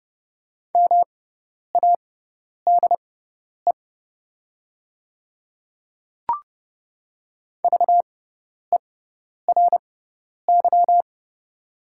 Morse code, heard as a keyed tone, about 30 wpm, spaced as for 10 wpm, 700 Hz. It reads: MADE VERY
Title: 6x Character Spacing - Top 200 Words - 30wpm